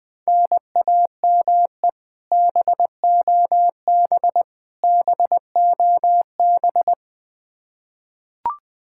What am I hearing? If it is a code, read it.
Morse code, heard as a keyed tone, about 21 wpm, 700 Hz. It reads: NAME BOB BOB
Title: K1USN SST QSO Practice 20wpm